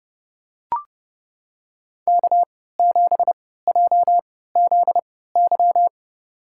Morse code, heard as a keyed tone, about 30 wpm, spaced as for 17 wpm, 700 Hz. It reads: K7JZY
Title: USA Call Signs - 3x Character Spacing - 30wpm